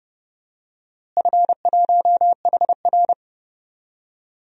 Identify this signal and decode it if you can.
Morse code, heard as a keyed tone, about 30 wpm, 700 Hz. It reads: F1HR
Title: DXCC-100 Call Signs - 4 Characters - Repeated 3x - 30wpm